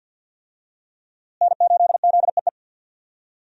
Morse code, heard as a keyed tone, about 52 wpm, 700 Hz. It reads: N8ZEE
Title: USA Call Signs - 5 Characters - Repeated 3x - 50wpm